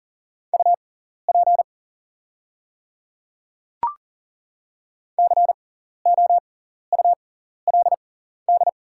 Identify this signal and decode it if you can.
Morse code, heard as a keyed tone, about 40 wpm, spaced as for 14 wpm, 700 Hz. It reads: UP COULD
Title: 6x Character Spacing - Top 100 Words - 40wpm